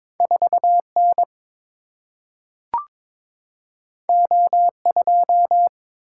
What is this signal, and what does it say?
Morse code, heard as a keyed tone, about 22 wpm, 700 Hz. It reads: 4N O2
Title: Single-Letter Single-Number 22wpm